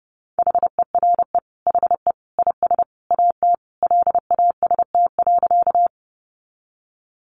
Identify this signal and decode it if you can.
Morse code, heard as a keyed tone, about 30 wpm, 700 Hz. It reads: HERE HE IS AT LAST.
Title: Sentences from Top 300 Words 30wpm